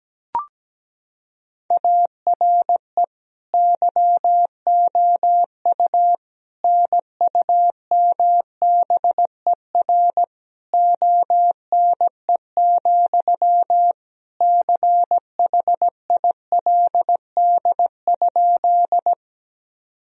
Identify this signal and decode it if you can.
Morse code, heard as a keyed tone, about 17 wpm, 700 Hz. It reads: ARE YOU NUMBER ONE, CHILD?
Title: Sentences from Top 1000 Words - Review Part 1 - 17wpm